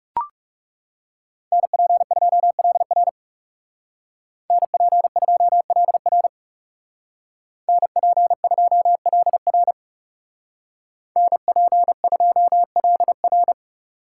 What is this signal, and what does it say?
Morse code, heard as a keyed tone, about 37 wpm, 700 Hz. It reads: NP2LR NP2LR NP2LR NP2LR
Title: Speed Racing - USA 2x2 Call Signs 30wpm